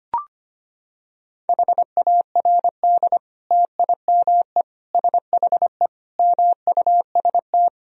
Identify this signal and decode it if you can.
Morse code, heard as a keyed tone, about 25 wpm, 700 Hz. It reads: HARD TIME SHE MUST
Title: Sentences from Top 1000 Words - Review Part 1 - 25wpm